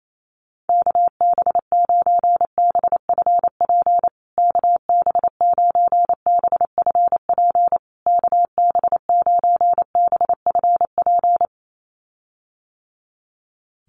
Morse code, heard as a keyed tone, about 28 wpm, 700 Hz. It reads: KB9BFP KB9BFP KB9BFP
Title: USA Call Signs Repeated 3x 28wpm